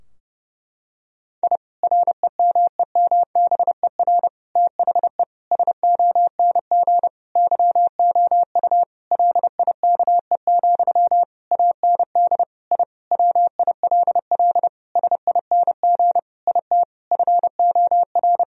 Morse code, heard as a keyed tone, about 30 wpm, 700 Hz. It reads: I REMEMBER THE SONG YOU LIKE, AND I WILL SING IT FOR
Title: Sentences from Top 1000 Words - Review Part 2 - 30wpm